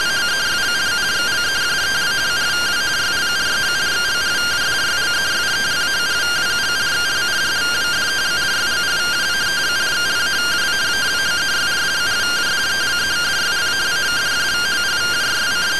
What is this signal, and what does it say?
Two-tone teleprinter signal, about 170 Hz shift.